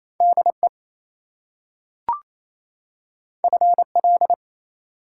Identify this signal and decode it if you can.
Morse code, heard as a keyed tone, about 28 wpm, 700 Hz. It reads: DE FL
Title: US States and Canadian Provinces 28wpm